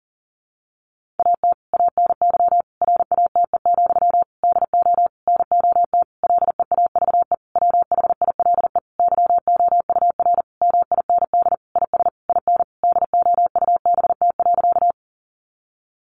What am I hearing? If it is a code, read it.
Morse code, heard as a keyed tone, about 40 wpm, 700 Hz. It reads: AT ANY RATE, DO NOT LEAVE WHILE YOUR MIND IS IN DOUBT.
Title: Sentences from Top 1500 Words 40wpm